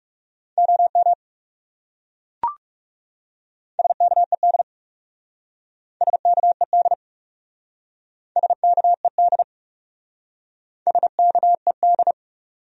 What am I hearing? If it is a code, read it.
Morse code, heard as a keyed tone, about 36 wpm, 700 Hz. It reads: OM SKED SKED SKED SKED
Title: Speed-Racing Abbreviations - 30wpm